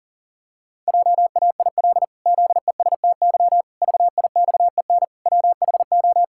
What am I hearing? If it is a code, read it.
Morse code, heard as a keyed tone, about 40 wpm, 700 Hz. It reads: JAIL ZESTY VIXEN WHO
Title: Pangram Challenge v2 - 40wpm